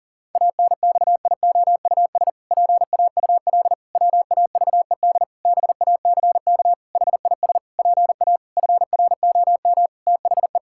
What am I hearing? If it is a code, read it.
Morse code, heard as a keyed tone, about 40 wpm, 700 Hz. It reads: ANXIOUS PAUL WAVED BACK HIS PA FROM THE
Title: Pangram Challenge v2 - 40wpm